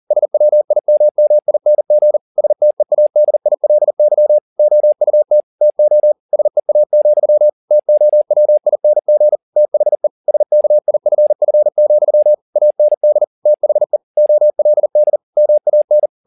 Morse code, heard as a keyed tone, about 40 wpm, 600 Hz. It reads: SWIMMING STEADILY OUT TO SEA, TOWING THE SKIFF, AND THE OLD MAN